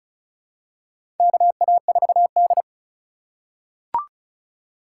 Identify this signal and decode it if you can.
Morse code, heard as a keyed tone, about 34 wpm, 700 Hz. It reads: KA4D
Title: USA Call Signs - 3x Character Spacing - 35wpm